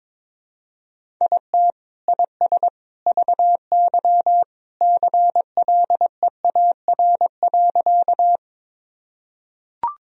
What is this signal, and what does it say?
Morse code, heard as a keyed tone, about 22 wpm, 700 Hz. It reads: IT IS VY CLEAR.
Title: Sentences from Top 500 Words and Abbreviations 22wpm